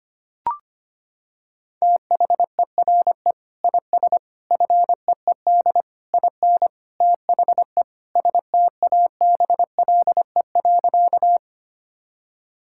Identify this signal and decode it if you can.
Morse code, heard as a keyed tone, about 25 wpm, 700 Hz. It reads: THERE IS FEED IN THE STABLE.
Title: Sentences from Top 1600 Words 25wpm